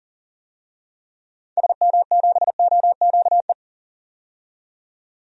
Morse code, heard as a keyed tone, about 40 wpm, 700 Hz. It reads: SM7OQE
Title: DXCC-100 Call Signs - 6 Characters - 40wpm